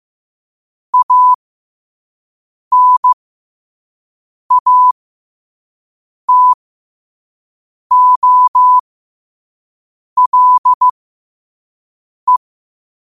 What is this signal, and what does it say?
Morse code, heard as a keyed tone, about 15 wpm, spaced as for 5 wpm, 1000 Hz. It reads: ANATOLE
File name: qotdfiles / 5wpm